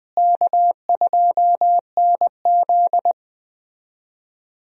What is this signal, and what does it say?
Morse code, heard as a keyed tone, about 20 wpm, 700 Hz. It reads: K2NZ
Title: USA Call Signs - 4 Characters - Repeated 3x - 20wpm